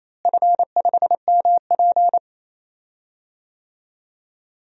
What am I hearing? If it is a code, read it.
Morse code, heard as a keyed tone, about 28 wpm, 700 Hz. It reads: F5MP
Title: DXCC-100 Call Signs - 4 Characters - Repeated 2x - 28wpm